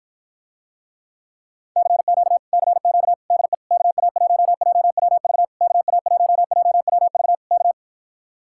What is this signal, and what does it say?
Morse code, heard as a keyed tone, about 54 wpm, 700 Hz. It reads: CQ CQ DE KA1JWV KA1JWV K